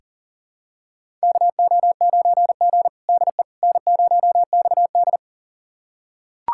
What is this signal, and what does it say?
Morse code, heard as a keyed tone, about 40 wpm, 700 Hz. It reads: KO9G DE N0XD
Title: USA Callsign DE Callsign - 4 Characters - 40wpm